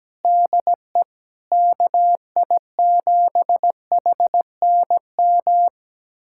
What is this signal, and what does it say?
Morse code, heard as a keyed tone, about 17 wpm, 700 Hz. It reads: DE KI7HNM
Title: USA Callsign DE Callsign - 6 Characters - 17wpm